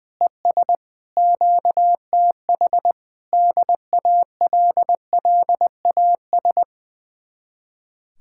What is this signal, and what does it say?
Morse code, heard as a keyed tone, about 20 wpm, 700 Hz. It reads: ES QTH DALLAS
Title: Speed-Racing Common QSO Phrases Encore - 20wpm